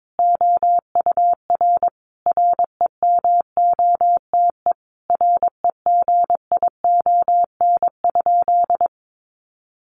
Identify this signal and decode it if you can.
Morse code, heard as a keyed tone, about 22 wpm, 700 Hz. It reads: OUR REMOTE REGION?
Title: Sentences from Top 1500 Words 22wpm